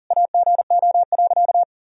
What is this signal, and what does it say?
Morse code, heard as a keyed tone, about 40 wpm, 700 Hz. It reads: AGO.
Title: Sentences from Top 500 Words and Abbreviations 40wpm